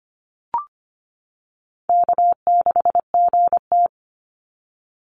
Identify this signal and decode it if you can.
Morse code, heard as a keyed tone, about 25 wpm, 700 Hz. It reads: K6GT
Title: Easy Call Signs - 25wpm